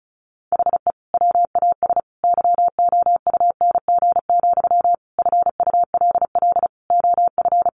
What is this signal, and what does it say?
Morse code, heard as a keyed tone, about 34 wpm, 700 Hz. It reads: HE WAS YOUNG, FULL OF
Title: Sentences from Top 1100 Words 35wpm